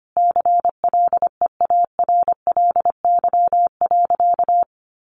Morse code, heard as a keyed tone, about 25 wpm, 700 Hz. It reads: CLEARLY.
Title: Sentences from Top 1300 Words 25wpm